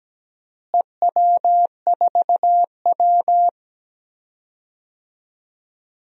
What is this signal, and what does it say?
Morse code, heard as a keyed tone, about 17 wpm, 700 Hz. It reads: EW4W
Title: DXCC-100 Call Signs - 4 Characters - Repeated 3x - 17wpm